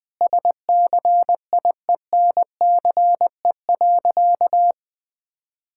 Morse code, heard as a keyed tone, about 20 wpm, 700 Hz. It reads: SCIENCE.
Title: Sentences from Top 1000 Words - Review Part 6 - 20wpm